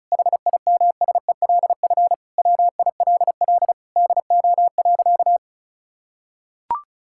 Morse code, heard as a keyed tone, about 34 wpm, 700 Hz. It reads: HIMSELF WILL DO.
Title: Sentences from Top 1600 Words 35wpm